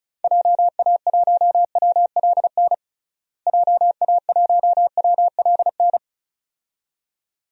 Morse code, heard as a keyed tone, about 36 wpm, 700 Hz. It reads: JA1WLN JA1WLN
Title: DXCC-100 Call Signs - 6 Characters - Repeated 2x - 35wpm